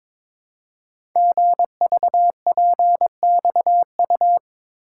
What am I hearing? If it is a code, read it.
Morse code, heard as a keyed tone, about 22 wpm, 700 Hz. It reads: GVPXU
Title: Code Groups - 5 Letters - 22wpm